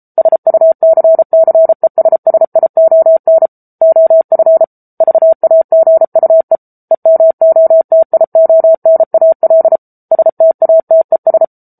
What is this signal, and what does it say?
Morse code, heard as a keyed tone, about 34 wpm, 650 Hz. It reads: SUCCESSION OF VAGUE EMOTIONAL STATES